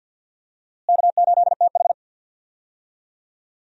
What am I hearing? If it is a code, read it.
Morse code, heard as a keyed tone, about 51 wpm, 700 Hz. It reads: K8TH
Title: USA Call Signs - 4 Characters - Repeated 3x - 50wpm